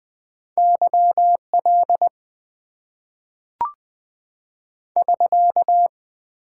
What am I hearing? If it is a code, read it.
Morse code, heard as a keyed tone, about 20 wpm, 700 Hz. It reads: Y L <SK>
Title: Top 90 QSO Elements - 8x Character Spacing - 20wpm